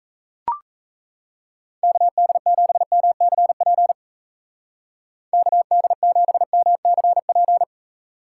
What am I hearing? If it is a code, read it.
Morse code, heard as a keyed tone, about 35 wpm, 700 Hz. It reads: KD7MCP KD7MCP
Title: Speed Racing - USA 2x3 Call Signs 28wpm